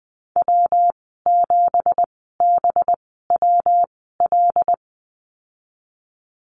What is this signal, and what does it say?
Morse code, heard as a keyed tone, about 20 wpm, spaced as for 15 wpm, 700 Hz. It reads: W7BWL
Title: 193 Lesson 31g - Letter v - Call Signs 20wpm